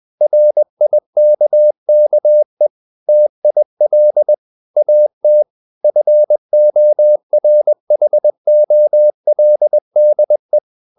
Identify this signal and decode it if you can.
Morse code, heard as a keyed tone, about 20 wpm, 600 Hz. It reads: RIKKE TIL AT FORHOLDE
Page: Ferie 20 wpm